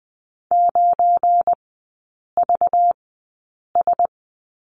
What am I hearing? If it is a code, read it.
Morse code, heard as a keyed tone, about 20 wpm, spaced as for 8 wpm, 700 Hz. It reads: 9VS
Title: 198 Lesson 32e - Letter k - Three Characters 20wpm